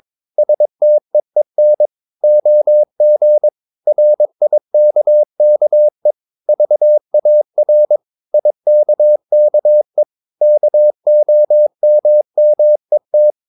Morse code, heard as a keyed tone, about 22 wpm, 600 Hz. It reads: STEEN OG RIKKE VAR IKKE KOMMET
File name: Ferie QRS 22wpm